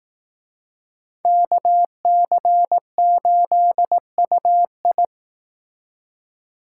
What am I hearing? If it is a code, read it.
Morse code, heard as a keyed tone, about 18 wpm, 700 Hz. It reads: KC8UI
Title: Speed Racing - USA 2x2 Call Signs 15wpm